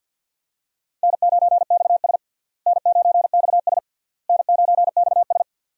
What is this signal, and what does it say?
Morse code, heard as a keyed tone, about 52 wpm, 700 Hz. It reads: N9XS N9XS N9XS
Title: USA Call Signs - 4 Characters - Repeated 3x - 50wpm